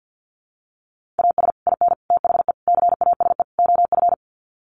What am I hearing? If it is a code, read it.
Morse code, heard as a keyed tone, about 48 wpm, 700 Hz. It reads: AS IN THE CASE OF
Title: Sets of 5 Words 50wpm